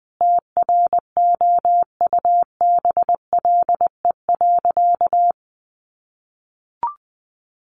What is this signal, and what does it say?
Morse code, heard as a keyed tone, about 20 wpm, 700 Hz. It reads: TROUBLE.
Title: Sentences from Top 1200 Words 20wpm